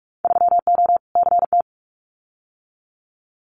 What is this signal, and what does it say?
Morse code, heard as a keyed tone, about 46 wpm, 700 Hz. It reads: 3O CT
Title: Winter Field Day Exchanges 45wpm